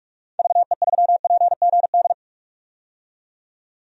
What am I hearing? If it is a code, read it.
Morse code, heard as a keyed tone, about 46 wpm, 700 Hz. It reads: VE3PGD
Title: DXCC-100 Call Signs - 6 Characters - 45wpm